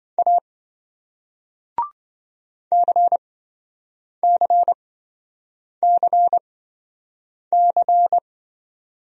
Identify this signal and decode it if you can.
Morse code, heard as a keyed tone, about 27 wpm, spaced as for 7 wpm, 700 Hz. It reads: A CCCC